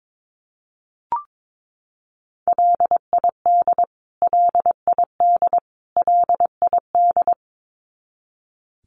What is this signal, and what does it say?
Morse code, heard as a keyed tone, about 22 wpm, 700 Hz. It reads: LID LID LID
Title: Abbreviations Repeated 3x - 22wpm